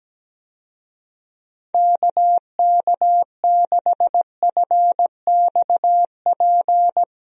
Morse code, heard as a keyed tone, about 17 wpm, 700 Hz. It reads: KK6FXP
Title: USA Call Signs - 6 Characters - Repeated 2x - 17wpm